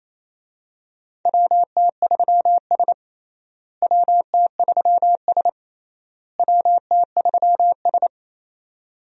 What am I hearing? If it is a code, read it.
Morse code, heard as a keyed tone, about 27 wpm, 700 Hz. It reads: WT3S WT3S WT3S
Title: USA Call Signs - 4 Characters - Repeated 3x - 28wpm